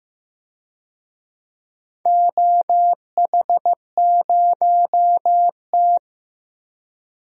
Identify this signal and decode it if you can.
Morse code, heard as a keyed tone, about 15 wpm, 700 Hz. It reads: OH0T
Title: DXCC-100 Call Signs - 4 Characters - Repeated 2x - 15wpm